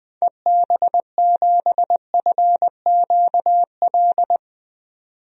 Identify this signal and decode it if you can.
Morse code, heard as a keyed tone, about 20 wpm, 700 Hz. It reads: EB7FQL
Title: DXCC-100 Call Signs - 6 Characters - 20wpm